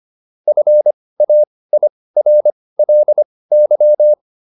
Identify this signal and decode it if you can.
Morse code, heard as a keyed tone, about 25 wpm, spaced as for 18 wpm, 600 Hz. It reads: FAIRLY